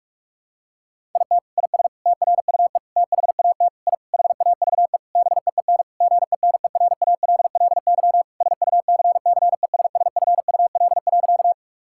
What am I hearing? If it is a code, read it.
Morse code, heard as a keyed tone, about 42 wpm, 700 Hz. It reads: IT IS TRUE THAT I HAVE BEEN GENERALLY SUCCESSFUL.
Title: Sentences from Top 1000 Words - Review Part 6 - 45wpm